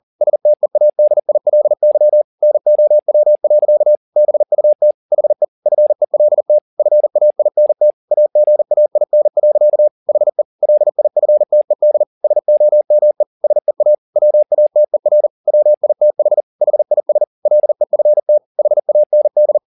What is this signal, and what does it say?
Morse code, heard as a keyed tone, about 40 wpm, 600 Hz. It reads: STEADILY NOW. BUT HE FELT FAINT AGAIN. HE LIFTED SOME SEA WATER WITH HIS LEFT HAND
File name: the old man and the sea 2 40wpm